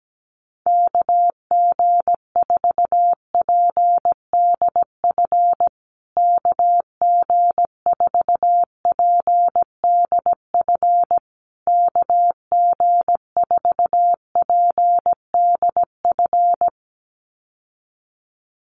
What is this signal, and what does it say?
Morse code, heard as a keyed tone, about 17 wpm, 700 Hz. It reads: KG4PDF KG4PDF KG4PDF
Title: USA Call Signs Repeated 3x 17wpm